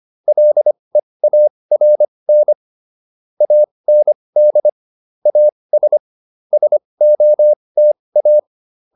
Morse code, heard as a keyed tone, about 25 wpm, spaced as for 20 wpm, 600 Hz. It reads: LEARN AND AS SOTA